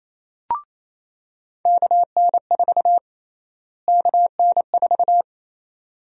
Morse code, heard as a keyed tone, about 28 wpm, 700 Hz. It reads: KN4 KN4
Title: USA Call Sign Prefixes - Repeated 2x - 28wpm